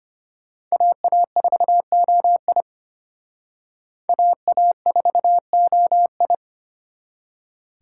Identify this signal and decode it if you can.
Morse code, heard as a keyed tone, about 29 wpm, 700 Hz. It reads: AA4OI AA4OI
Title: Speed Racing - USA 2x2 Call Signs 25wpm